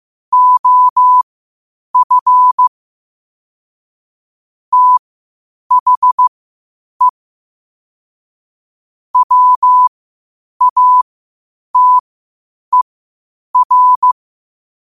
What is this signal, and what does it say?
Morse code, heard as a keyed tone, about 15 wpm, spaced as for 8 wpm, 1000 Hz. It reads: OF THE WATER
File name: qotdfiles / 8wpm